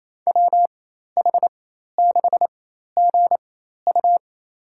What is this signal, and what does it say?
Morse code, heard as a keyed tone, about 28 wpm, spaced as for 13 wpm, 700 Hz. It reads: WH6GU
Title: USA Call Signs - 5 Characters - 4x Character Spacing - 28wpm